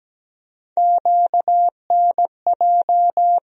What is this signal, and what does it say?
Morse code, heard as a keyed tone, about 17 wpm, 700 Hz. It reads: QNJ
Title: USA Call Sign Postfixes - 3 Letters 17wpm